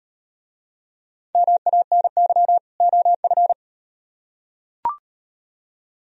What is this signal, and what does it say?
Morse code, heard as a keyed tone, about 38 wpm, 700 Hz. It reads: MANY OF